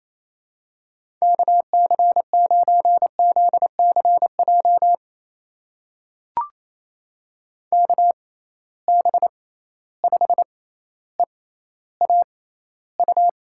Morse code, heard as a keyed tone, about 28 wpm, spaced as for 14 wpm, 700 Hz. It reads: KC9ZCJ KB5EAU
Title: USA Call Signs - 6 Characters - 6x Character Spacing - 28wpm